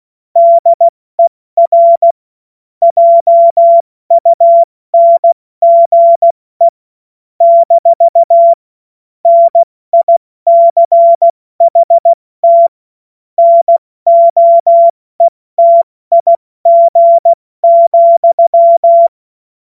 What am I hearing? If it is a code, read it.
Morse code, heard as a keyed tone, about 16 wpm, 680 Hz. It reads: DER JUNGE - NICHT NOETIG,
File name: Morsekurs / Witze / Witz 16. Februar 2021, 70 F80